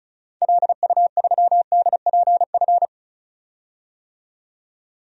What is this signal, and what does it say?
Morse code, heard as a keyed tone, about 34 wpm, 700 Hz. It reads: LU3DPF
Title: DXCC-100 Call Signs - 6 Characters - 35wpm